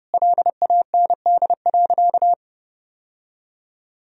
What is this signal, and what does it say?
Morse code, heard as a keyed tone, about 32 wpm, 700 Hz. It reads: LAND.
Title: Sentences from Top 1000 Words - Review Part 4 - 30wpm